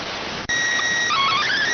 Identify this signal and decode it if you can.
SSTV (slow-scan television), PD 290.